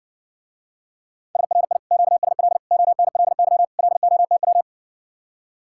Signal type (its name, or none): Morse code